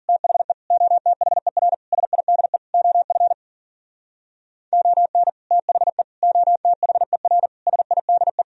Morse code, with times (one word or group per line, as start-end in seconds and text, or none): 0.1-0.5 THE
0.7-1.7 OTHER
1.9-2.6 SIDE
2.7-3.3 OF
4.7-5.3 ON
5.5-6.0 THE
6.2-7.5 OTHER
7.7-8.4 SIDE